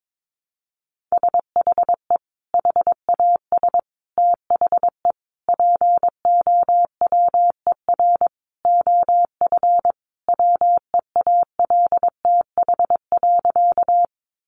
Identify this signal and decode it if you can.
Morse code, heard as a keyed tone, about 22 wpm, 700 Hz. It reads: SHE HAS THE POWER OF WEALTH.
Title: Sentences from Top 1200 Words 22wpm